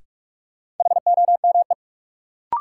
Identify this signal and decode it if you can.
Morse code, heard as a keyed tone, about 45 wpm, 700 Hz. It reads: HOME
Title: Sentences from Top 200 Words - New Word Drill - 45wpm